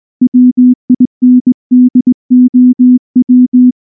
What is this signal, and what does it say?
Morse code, heard as a keyed tone, about 19 wpm, 250 Hz. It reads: WINDOW
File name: morse / morse-training-20WPM